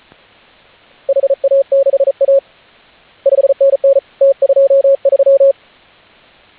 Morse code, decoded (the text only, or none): HA6A 5NN T23